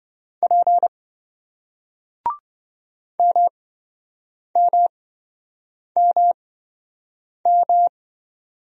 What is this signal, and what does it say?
Morse code, heard as a keyed tone, about 27 wpm, spaced as for 7 wpm, 700 Hz. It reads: P MMMM